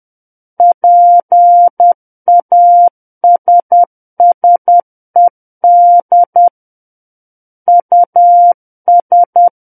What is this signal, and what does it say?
Morse code, heard as a keyed tone, about 10 wpm, 700 Hz. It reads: PASSED US
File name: Book One / 01-Chapter 1 - 10 WPM